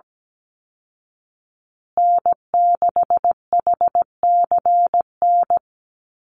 Morse code, heard as a keyed tone, about 17 wpm, 700 Hz. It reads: N6HCN
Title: Call Signs Repeated 2x - 17wpm